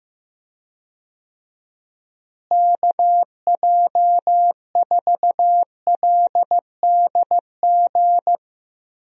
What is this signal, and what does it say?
Morse code, heard as a keyed tone, about 15 wpm, 700 Hz. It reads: KJ4LDG
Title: USA Call Signs - 6 Characters - Repeated 3x - 15wpm